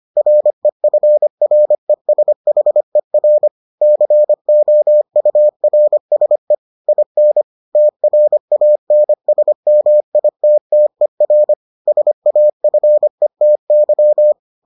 Morse code, heard as a keyed tone, about 25 wpm, 600 Hz. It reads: REFRESHER COURSE IN TRANSMITTER SAFETY